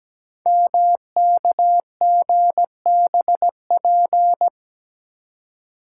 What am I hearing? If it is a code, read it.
Morse code, heard as a keyed tone, about 17 wpm, 700 Hz. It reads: MKGBP
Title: Code Groups - 5 Letters - 17wpm